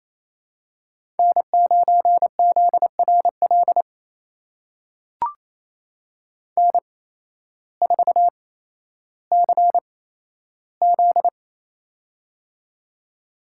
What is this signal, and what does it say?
Morse code, heard as a keyed tone, about 28 wpm, spaced as for 13 wpm, 700 Hz. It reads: N9ZRL N4CZ
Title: USA Call Signs - 8x Character Spacing - 28wpm